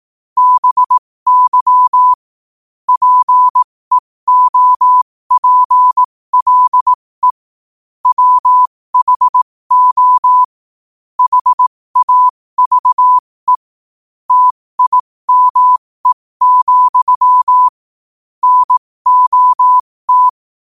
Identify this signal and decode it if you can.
Morse code, heard as a keyed tone, about 18 wpm, spaced as for 15 wpm, 1000 Hz. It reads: BY PEOPLE WHO HAVE TIME, NOT
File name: qotdfiles / 15wpm